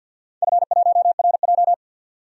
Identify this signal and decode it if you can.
Morse code, heard as a keyed tone, about 47 wpm, 700 Hz. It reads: F1RJ
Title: DXCC-100 Call Signs - 4 Characters - 50wpm